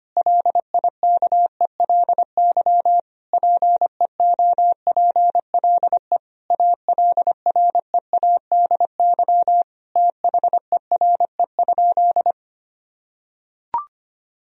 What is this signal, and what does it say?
Morse code, heard as a keyed tone, about 25 wpm, 700 Hz. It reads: LIKELY PEOPLE ALREADY THERE?
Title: Sentences from Top 1000 Words - Review Part 4 - 25wpm